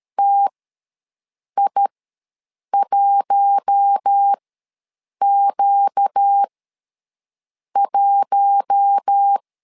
Morse code, heard as a keyed tone, about 13 wpm, spaced as for 6 wpm, 800 Hz. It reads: TI1Q1
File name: LEZIONI / 4 Q1 ws 12wpm 2129